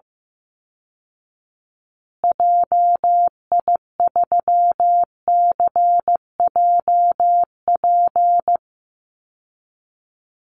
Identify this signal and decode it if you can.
Morse code, heard as a keyed tone, about 14 wpm, 700 Hz. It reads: JI3CJP
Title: Call Signs Repeated 2x - 15wpm